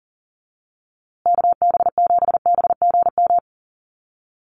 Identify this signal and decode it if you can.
Morse code, heard as a keyed tone, about 38 wpm, 700 Hz. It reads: KB7BGM
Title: USA Call Signs Repeated 3x 40wpm